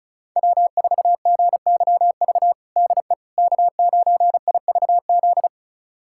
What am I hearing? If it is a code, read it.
Morse code, heard as a keyed tone, about 36 wpm, 700 Hz. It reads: W4GYV DE K9IVZ
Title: USA Callsign DE Callsign - 5 Characters - 35wpm